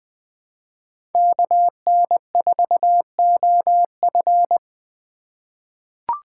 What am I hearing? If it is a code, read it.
Morse code, heard as a keyed tone, about 20 wpm, 700 Hz. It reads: KN4OF
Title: USA Call Signs - 3x Character Spacing - 20wpm